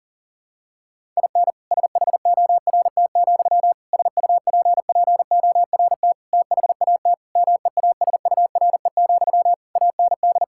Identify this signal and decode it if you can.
Morse code, heard as a keyed tone, about 37 wpm, 700 Hz. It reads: IN SHORT, SUPPORT THAT MEASURE, AND
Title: Sentences from Top 1000 Words - Review Part 5 - 40wpm